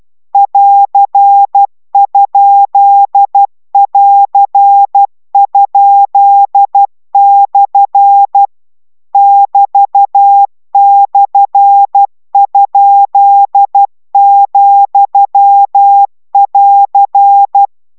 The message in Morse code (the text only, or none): <AR>?<AR>?/ <BT>/?,<AR>